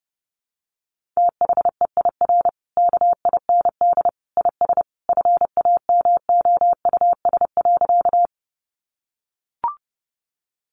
Morse code, heard as a keyed tone, about 30 wpm, 700 Hz. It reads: THEIR KIND IS FAMOUS.
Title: Sentences from Top 1500 Words 30wpm